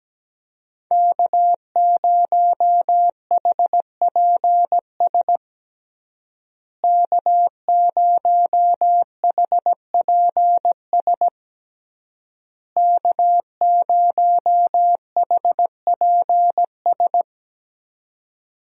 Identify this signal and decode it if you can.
Morse code, heard as a keyed tone, about 17 wpm, 700 Hz. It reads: K0HPS K0HPS K0HPS
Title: USA Call Signs - 5 Characters - Repeated 3x - 17wpm